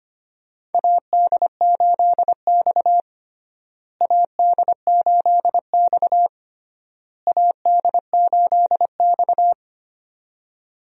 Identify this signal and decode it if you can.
Morse code, heard as a keyed tone, about 25 wpm, 700 Hz. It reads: AD8X AD8X AD8X
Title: USA Call Signs - 4 Characters - Repeated 3x - 25wpm